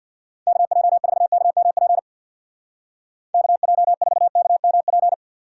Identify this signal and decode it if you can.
Morse code, heard as a keyed tone, about 66 wpm, 700 Hz. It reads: KJ4KMP KJ4KMP